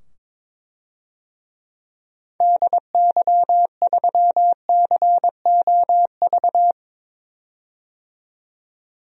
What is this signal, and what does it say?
Morse code, heard as a keyed tone, about 21 wpm, 700 Hz. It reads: DY3COV
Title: DXCC-100 Call Signs - 6 Characters - 22wpm